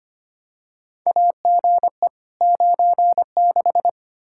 Morse code, heard as a keyed tone, about 25 wpm, 700 Hz. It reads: AGE 96
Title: Practice QSO - Exchange 3 - More Info - 25wpm